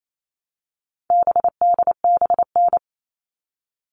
Morse code, heard as a keyed tone, about 28 wpm, 700 Hz. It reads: BDBN
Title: All Cut Numbers - 4-Digit 28wpm